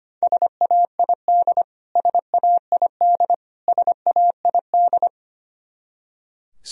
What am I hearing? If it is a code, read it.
Morse code, heard as a keyed tone, about 25 wpm, 700 Hz. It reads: SAID SAID SAID
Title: Top 100 Words Repeated 25wpm